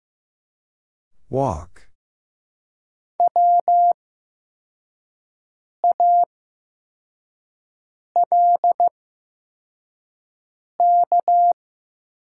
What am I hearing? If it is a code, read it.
Morse code, heard as a keyed tone, about 15 wpm, spaced as for 4 wpm, 700 Hz. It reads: EWALK
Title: 8x Character Spacing - Top 500 Words - 15wpm